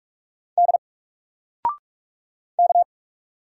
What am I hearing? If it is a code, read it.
Morse code, heard as a keyed tone, about 44 wpm, 700 Hz. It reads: D K
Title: Single Letter-Number Alt 45wpm